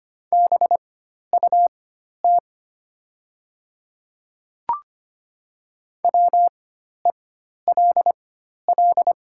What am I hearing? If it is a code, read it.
Morse code, heard as a keyed tone, about 25 wpm, spaced as for 12 wpm, 690 Hz. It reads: BUT WELL
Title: 4x Character Spacing - Top 300 Words - 25wpm